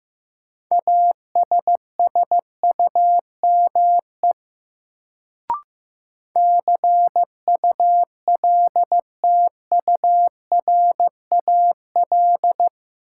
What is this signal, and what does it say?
Morse code, heard as a keyed tone, about 15 wpm, 700 Hz. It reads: ASSUME CULTURAL
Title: Sentences from Top 1500 Words - New Word Drill - 15wpm